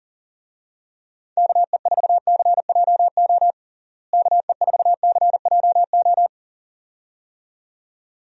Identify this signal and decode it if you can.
Morse code, heard as a keyed tone, about 37 wpm, 700 Hz. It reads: KE4CJO KE4CJO
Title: USA Call Signs - 6 Characters - Repeated 2x - 40wpm